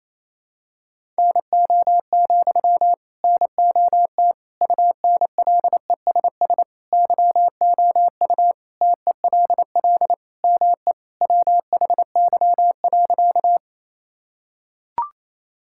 Morse code, heard as a keyed tone, about 28 wpm, 700 Hz. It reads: NO, NOT UNLESS YOU TELL ME WHY.
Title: Sentences from Top 1000 Words - Review Part 6 - 28wpm